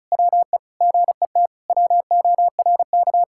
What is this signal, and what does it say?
Morse code, heard as a keyed tone, about 34 wpm, 700 Hz. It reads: WE GET WORK
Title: Sentences from Top 1000 Words - Review Part 6 - 35wpm